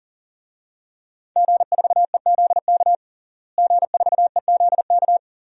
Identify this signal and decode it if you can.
Morse code, heard as a keyed tone, about 40 wpm, 700 Hz. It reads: G4EZK G4EZK
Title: DXCC-100 Call Signs - 5 Characters - Repeated 2x - 40wpm